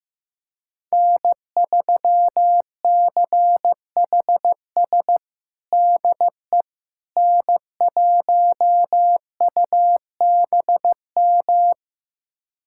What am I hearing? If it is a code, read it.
Morse code, heard as a keyed tone, about 15 wpm, 700 Hz. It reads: N3CHS DE N1UBM
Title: USA Callsign DE Callsign - 5 Characters - 15wpm